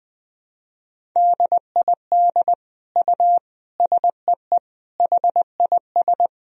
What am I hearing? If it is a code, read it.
Morse code, heard as a keyed tone, about 20 wpm, 700 Hz. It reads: DID U SEE HIS
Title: Sentences from Top 300 Words and Abbreviations 20wpm